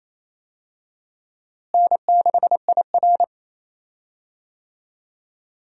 Morse code, heard as a keyed tone, about 28 wpm, 700 Hz. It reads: N6IR
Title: USA Call Signs - 4 Characters - Repeated 2x - 28wpm